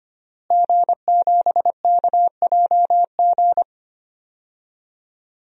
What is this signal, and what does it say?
Morse code, heard as a keyed tone, about 25 wpm, 700 Hz. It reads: G7KJG
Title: DXCC-100 Call Signs - 5 Characters - Repeated 3x - 25wpm